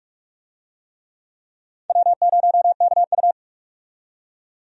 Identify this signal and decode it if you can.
Morse code, heard as a keyed tone, about 43 wpm, 700 Hz. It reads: W0KU
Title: USA Call Signs - 4 Characters - 45wpm